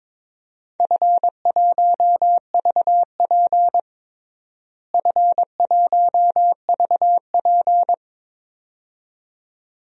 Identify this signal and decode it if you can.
Morse code, heard as a keyed tone, about 22 wpm, 700 Hz. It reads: F1VP F1VP
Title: DXCC-100 Call Signs - 4 Characters - Repeated 2x - 22wpm